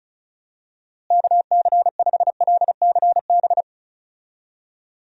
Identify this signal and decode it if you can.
Morse code, heard as a keyed tone, about 36 wpm, 700 Hz. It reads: KC5LCB
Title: USA Call Signs - 6 Characters - Repeated 2x - 35wpm